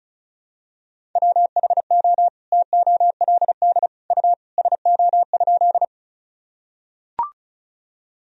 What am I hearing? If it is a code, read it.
Morse code, heard as a keyed tone, about 34 wpm, 700 Hz. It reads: WHO TOLD U SO?
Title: Sentences from Top 400 Words and Abbreviations 35wpm